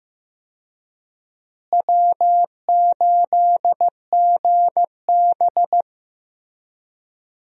Morse code, heard as a keyed tone, about 16 wpm, 700 Hz. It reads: W8GB
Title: USA Call Signs - 4 Characters - Repeated 2x - 15wpm